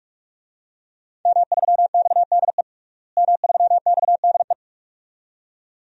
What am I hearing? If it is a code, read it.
Morse code, heard as a keyed tone, about 44 wpm, 700 Hz. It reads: M3XDE M3XDE
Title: DXCC-100 Call Signs - 5 Characters - Repeated 2x - 45wpm